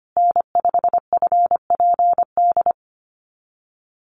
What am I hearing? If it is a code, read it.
Morse code, heard as a keyed tone, about 25 wpm, 700 Hz. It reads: N5FPD
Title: USA Call Signs Repeated 3x 25wpm